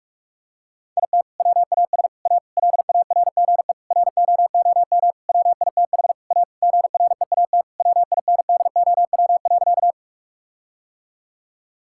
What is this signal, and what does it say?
Morse code, heard as a keyed tone, about 46 wpm, 700 Hz. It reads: IT WAS A LARGE ROOM WITH A GREAT WINDOW.
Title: Sentences from Top 1600 Words 45wpm